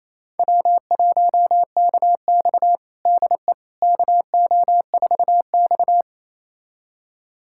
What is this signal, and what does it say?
Morse code, heard as a keyed tone, about 28 wpm, 700 Hz. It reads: W1KX DE KO4X
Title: USA Callsign DE Callsign - 4 Characters - 28wpm